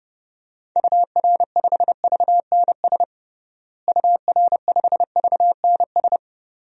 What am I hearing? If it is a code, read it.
Morse code, heard as a keyed tone, about 30 wpm, 700 Hz. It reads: UR5VNS UR5VNS
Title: DXCC-100 Call Signs - 6 Characters - Repeated 2x - 30wpm